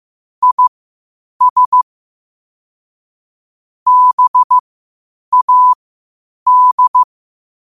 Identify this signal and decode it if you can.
Morse code, heard as a keyed tone, about 15 wpm, spaced as for 8 wpm, 1000 Hz. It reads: IS BAD
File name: qotdfiles / 8wpm